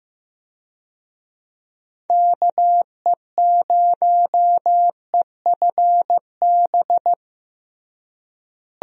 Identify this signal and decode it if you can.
Morse code, heard as a keyed tone, about 15 wpm, 700 Hz. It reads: KE0EFB
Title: USA Call Signs - 6 Characters - Repeated 3x - 15wpm